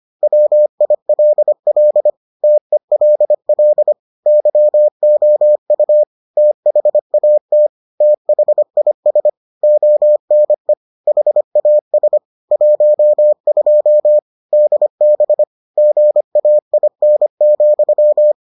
Morse code, heard as a keyed tone, about 25 wpm, 600 Hz. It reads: WILL TELL YOU THAT THIS ONE HAS 12 DB GAIN,